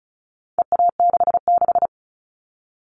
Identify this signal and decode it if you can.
Morse code, heard as a keyed tone, about 34 wpm, 700 Hz. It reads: EA66
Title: All Cut Numbers - 4-Digit 35wpm